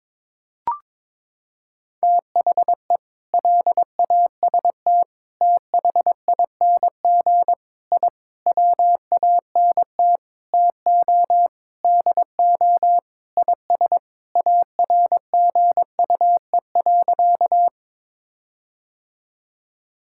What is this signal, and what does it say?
Morse code, heard as a keyed tone, about 21 wpm, 700 Hz. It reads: THE LAST THING I WANT TO DO IS ARGUE.
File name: Sentences with 7-Word Phrases - 22wpm